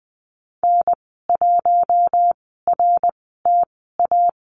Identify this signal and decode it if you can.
Morse code, heard as a keyed tone, about 20 wpm, spaced as for 14 wpm, 700 Hz. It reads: N1RTA
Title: 088 Lesson 15g - Number 5 - Call Signs 20wpm